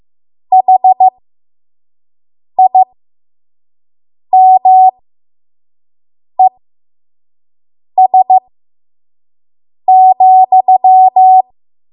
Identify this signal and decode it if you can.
Morse code, heard as a keyed tone, about 15 wpm, spaced as for 5 wpm, 750 Hz. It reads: HIMES,
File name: Morse / 05WPM